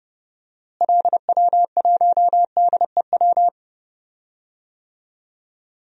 Morse code, heard as a keyed tone, about 31 wpm, 700 Hz. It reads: LW1DEW
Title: DXCC-100 Call Signs - 6 Characters - Repeated 2x - 30wpm